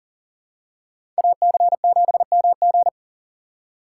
Morse code, heard as a keyed tone, about 40 wpm, 700 Hz. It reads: AC7MG
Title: USA Call Signs - 5 Characters - 3x Character Spacing - 40wpm